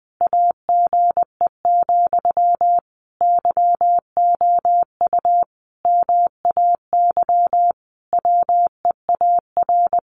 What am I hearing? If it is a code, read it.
Morse code, heard as a keyed tone, about 20 wpm, 700 Hz. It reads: AGE, YOU MAY WEAR
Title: Sentences from Top 1500 Words 20wpm